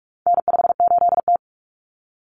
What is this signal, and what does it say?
Morse code, heard as a keyed tone, about 44 wpm, 700 Hz. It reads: N58T
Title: Common Cut Numbers - 4-Digit 45wpm